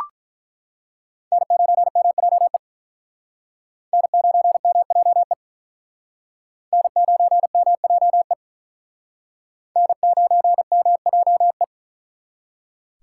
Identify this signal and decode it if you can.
Morse code, heard as a keyed tone, about 47 wpm, 700 Hz. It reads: N9MJE N9MJE N9MJE N9MJE